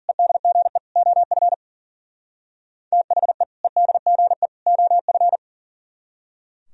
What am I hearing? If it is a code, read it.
Morse code, heard as a keyed tone, about 49 wpm, 700 Hz. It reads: EDGE OF THE EDGE OF
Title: Speed-Racing - Sets of 3 Words 40wpm